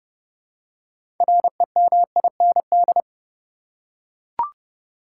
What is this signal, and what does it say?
Morse code, heard as a keyed tone, about 30 wpm, 700 Hz. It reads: REMIND
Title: Six Letter Words - 30wpm